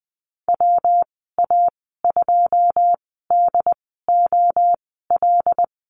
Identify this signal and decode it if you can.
Morse code, heard as a keyed tone, about 20 wpm, spaced as for 15 wpm, 700 Hz. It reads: WA2DOL
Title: 167 Lesson 27g - Number 7 - Call Signs 20wpm